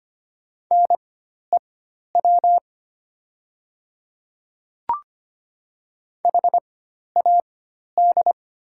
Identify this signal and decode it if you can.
Morse code, heard as a keyed tone, about 25 wpm, spaced as for 12 wpm, 700 Hz. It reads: NEW HAD
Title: 4x Character Spacing - Top 200 Words - 25wpm